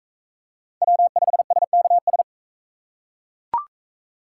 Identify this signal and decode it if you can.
Morse code, heard as a keyed tone, about 37 wpm, 700 Hz. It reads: W5SKS